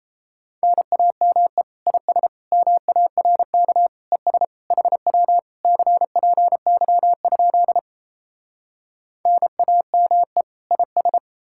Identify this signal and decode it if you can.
Morse code, heard as a keyed tone, about 39 wpm, 700 Hz. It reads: NAME IS MARK ES HW CPY? NAME IS